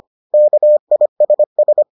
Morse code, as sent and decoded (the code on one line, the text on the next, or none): -.- .. ... ...
KISS